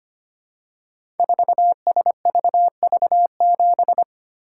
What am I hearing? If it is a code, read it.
Morse code, heard as a keyed tone, about 25 wpm, 700 Hz. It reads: 4SVV7
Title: Code Groups - 5 Letter:Numbers - 25wpm